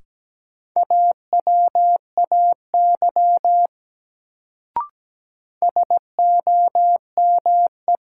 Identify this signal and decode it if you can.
Morse code, heard as a keyed tone, about 17 wpm, 700 Hz. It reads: AWAY SOME
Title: Sets of 2 Words - New Word Drill - 17wpm